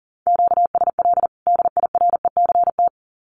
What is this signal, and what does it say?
Morse code, heard as a keyed tone, about 40 wpm, 700 Hz. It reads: QSL DIRECT
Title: Common QSO Phrases 40wpm